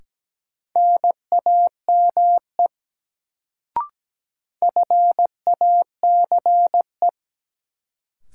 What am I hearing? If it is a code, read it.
Morse code, heard as a keyed tone, about 17 wpm, 700 Hz. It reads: NAME FACE
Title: Sets of 3 Words - New Word Drill - 17wpm